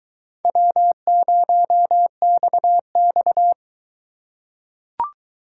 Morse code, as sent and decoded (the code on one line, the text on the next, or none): .-- ----- -..- -..-
W0XX